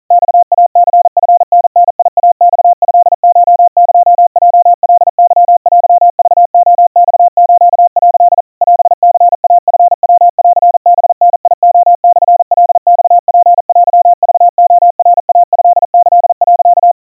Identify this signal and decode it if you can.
Morse code (Japanese Wabun code), heard as a keyed tone, about 40 wpm, 700 Hz. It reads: マイニチタダイマトコエヲカケテクレマス」 オニイチヤンハダレモカマツテクレナイトキ、
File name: kawa / Morse sample / kamyu 150 cpm 700 Hz Crispy